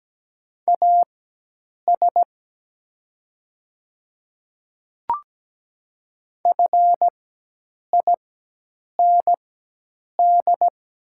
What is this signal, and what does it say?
Morse code, heard as a keyed tone, about 17 wpm, spaced as for 8 wpm, 700 Hz. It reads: AS FIND
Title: 4x Character Spacing - Top 200 Words - 17wpm